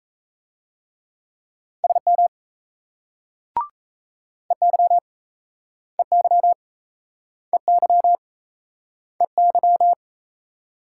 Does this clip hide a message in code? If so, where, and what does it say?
Yes: Morse code from 1.8 to 9.9 s, heard as a keyed tone, about 40 wpm, 700 Hz. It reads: SM EY EY EY EY